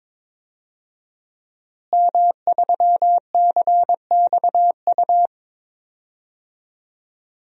Morse code, heard as a keyed tone, about 22 wpm, 700 Hz. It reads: M3CXU
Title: DXCC-100 Call Signs - 5 Characters - Repeated 3x - 22wpm